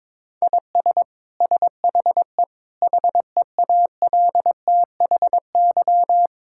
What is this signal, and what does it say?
Morse code, heard as a keyed tone, about 22 wpm, 700 Hz. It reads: IS SHE HEALTHY
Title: Sentences from Top 1000 Words - Review Part 5 - 22wpm